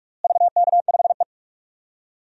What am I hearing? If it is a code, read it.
Morse code, heard as a keyed tone, about 42 wpm, 700 Hz. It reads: VK5E
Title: DXCC-100 Call Signs - 4 Characters - Repeated 2x - 45wpm